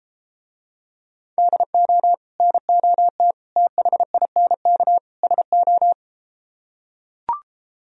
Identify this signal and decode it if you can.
Morse code, heard as a keyed tone, about 32 wpm, 700 Hz. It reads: DO NOT THINK SO